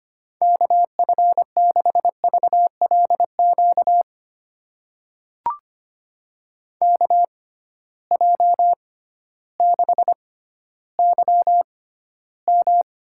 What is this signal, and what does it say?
Morse code, heard as a keyed tone, about 25 wpm, spaced as for 13 wpm, 700 Hz. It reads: KF6VLQ KJ6YM
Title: USA Call Signs - 6 Characters - 6x Character Spacing - 25wpm